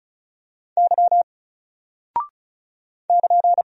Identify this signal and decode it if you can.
Morse code, heard as a keyed tone, about 36 wpm, 700 Hz. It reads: Y <KN>
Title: Single Character (Letter:Number:Punctuation:Prosign) 35wpm